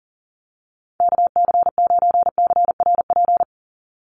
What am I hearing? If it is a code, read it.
Morse code, heard as a keyed tone, about 39 wpm, 700 Hz. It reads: KC9CRP
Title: USA Call Signs 40wpm